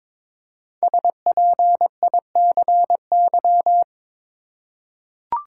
Morse code, heard as a keyed tone, about 22 wpm, 700 Hz. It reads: SPICY
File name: 5 Letter Words - First 3 Letters Given - 22wpm